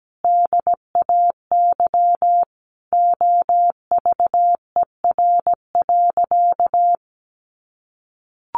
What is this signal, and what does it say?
Morse code, heard as a keyed tone, about 17 wpm, 700 Hz. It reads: DAY OVER.
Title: A Sound Of Thunder 17wpm